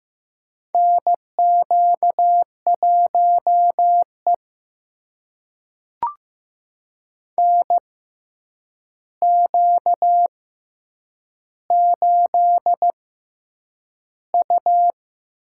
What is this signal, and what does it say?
Morse code, heard as a keyed tone, about 15 wpm, spaced as for 8 wpm, 700 Hz. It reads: NQ1E NQ8U
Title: USA Call Signs - 4 Characters - 6x Character Spacing - 15wpm